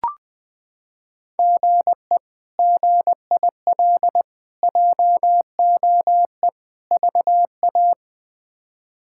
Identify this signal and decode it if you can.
Morse code, heard as a keyed tone, about 20 wpm, 700 Hz. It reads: GE GIL JOE VA
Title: K1USN SST QSO Practice - 20wpm